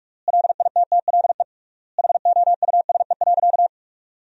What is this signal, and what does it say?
Morse code, heard as a keyed tone, about 46 wpm, 700 Hz. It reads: LITTLE HOUSE.
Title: 3x Word-Spacing - Sentences from Top 200 Words - 45wpm